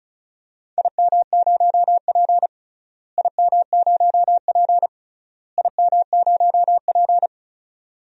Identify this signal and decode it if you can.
Morse code, heard as a keyed tone, about 34 wpm, 700 Hz. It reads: IM0P IM0P IM0P
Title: DXCC-100 Call Signs - 4 Characters - Repeated 3x - 35wpm